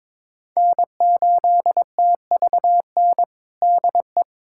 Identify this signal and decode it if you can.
Morse code, heard as a keyed tone, about 22 wpm, 700 Hz. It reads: N8TVN DE
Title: USA Callsign DE Callsign 22wpm